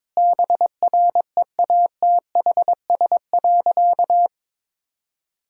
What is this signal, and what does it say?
Morse code, heard as a keyed tone, about 22 wpm, 700 Hz. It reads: BREATHS.
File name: practice / 22wpm / Easy Sentences with Hints - 22wpm